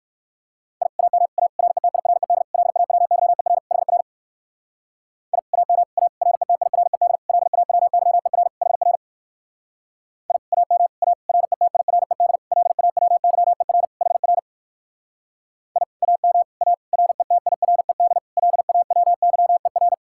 Morse code, heard as a keyed tone, about 56 wpm, 700 Hz. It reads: I AM A RETIRED LAWYER HR I AM A RETIRED LAWYER HR I AM A RETIRED LAWYER HR I AM A RETIRED LAWYER
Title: Speed-Racing Common QSO Phrases Encore - 45wpm